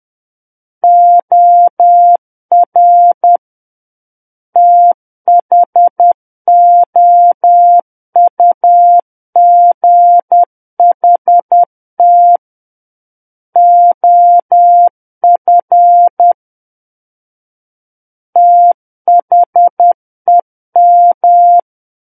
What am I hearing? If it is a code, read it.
Morse code, heard as a keyed tone, about 10 wpm, 700 Hz. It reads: OR THOUGHT OF THEM
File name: Book One / 01-Chapter 1 - 10 WPM